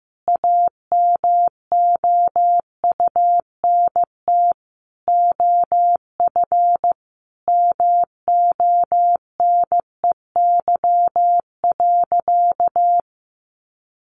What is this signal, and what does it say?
Morse code, heard as a keyed tone, about 15 wpm, 700 Hz. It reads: AMOUNT OF MONEY.
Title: Sentences from Top 1300 Words 15wpm